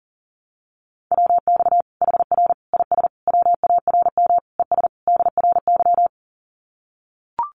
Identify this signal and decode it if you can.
Morse code, heard as a keyed tone, about 40 wpm, 700 Hz. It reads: WX HR IS WARM ES DRY
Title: Common QSO Phrases 40wpm